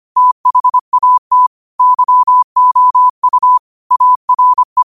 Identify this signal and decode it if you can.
Morse code, heard as a keyed tone, about 25 wpm, 1000 Hz. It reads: THAT YOU ARE
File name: qotdfiles / 25wpm